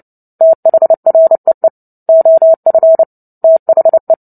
Morse code, heard as a keyed tone, about 30 wpm, 650 Hz. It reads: THREE OF THE
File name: Book Two / 03-Chapter 3 - 29 WPM